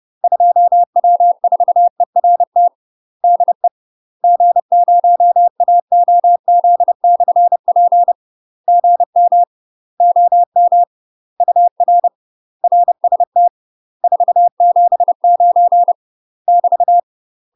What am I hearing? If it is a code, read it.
Morse code, heard as a keyed tone, about 30 wpm, 700 Hz. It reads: 2W4ERT DE G0AOZ/P GM OM UR RST 479 <BT>
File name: UK MASTER QSOs TRAINING / 30WPM / UK MASTER QSOs 0002